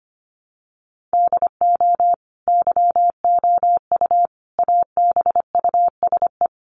Morse code, heard as a keyed tone, about 25 wpm, 700 Hz. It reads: DO YOU ABUSE
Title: Sentences from Top 1300 Words 25wpm